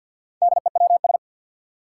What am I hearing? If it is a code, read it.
Morse code, heard as a keyed tone, about 47 wpm, 700 Hz. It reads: DEWS
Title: Four Challenging Letter-Number Combinations 50wpm